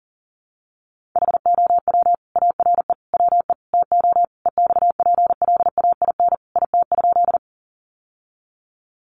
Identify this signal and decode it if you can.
Morse code, heard as a keyed tone, about 40 wpm, 700 Hz. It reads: HOW ARE WE TO EXPLAIN IT?
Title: Sentences from Top 1100 Words 40wpm